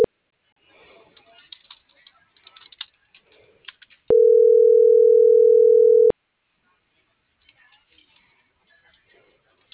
Ringback tone (North America), 440 + 480 Hz.